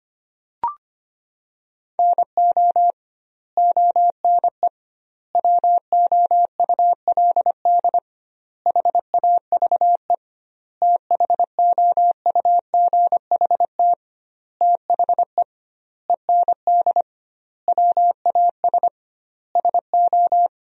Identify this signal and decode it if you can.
Morse code, heard as a keyed tone, about 25 wpm, 700 Hz. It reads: NO ONE WOULD HAVE THOUGHT THE END WAS SO
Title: 2x Word-Spacing - Sentences from Top 300 Words - 25wpm